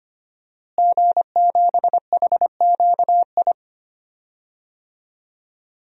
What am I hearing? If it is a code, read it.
Morse code, heard as a keyed tone, about 25 wpm, 700 Hz. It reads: G7HQI
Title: DXCC-100 Call Signs - 5 Characters - Repeated 2x - 25wpm